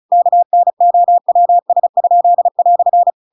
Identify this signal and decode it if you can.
Morse code, heard as a keyed tone, about 35 wpm, 700 Hz. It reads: KNOWS?"
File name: Pinocchio / Pinocchio ch7 35 35 ik1hge